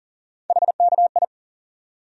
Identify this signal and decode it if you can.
Morse code, heard as a keyed tone, about 40 wpm, 700 Hz. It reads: HKI
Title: USA Call Sign Postfixes - 3 Letters 40wpm